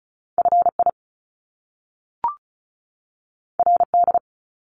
Morse code, heard as a keyed tone, about 36 wpm, 700 Hz. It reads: FI RD
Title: Top Two Letter Combinations in Words 35wpm